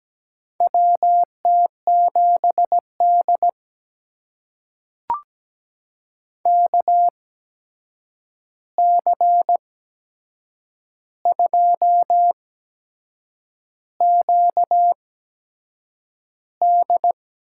Morse code, heard as a keyed tone, about 17 wpm, spaced as for 7 wpm, 700 Hz. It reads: WT7D KC2QD
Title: USA Call Signs - 8x Character Spacing - 17wpm